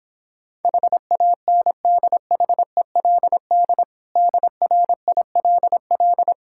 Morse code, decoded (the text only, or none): HANDHELD DRILL